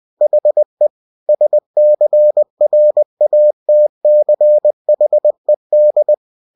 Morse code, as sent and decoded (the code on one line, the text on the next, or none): .... / ... -.-. .-. .- - -.-. .... -..
HE SCRATCHED